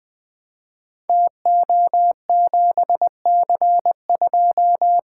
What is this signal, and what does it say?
Morse code, heard as a keyed tone, about 20 wpm, 700 Hz. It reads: TO7C2
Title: Code Groups - 5 Letter:Numbers - 20wpm